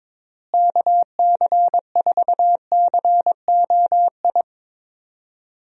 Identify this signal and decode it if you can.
Morse code, heard as a keyed tone, about 22 wpm, 700 Hz. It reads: KC4COI
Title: USA Call Signs - 6 Characters - Repeated 2x - 22wpm